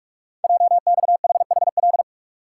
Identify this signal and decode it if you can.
Morse code, heard as a keyed tone, about 44 wpm, 700 Hz. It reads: JXHHL
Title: Code Groups - 5 Letters - 45wpm